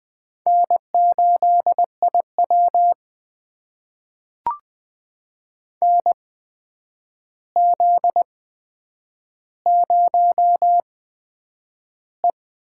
Morse code, heard as a keyed tone, about 20 wpm, spaced as for 9 wpm, 700 Hz. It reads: N8IW NZ0E
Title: USA Call Signs - 4 Characters - 8x Character Spacing - 20wpm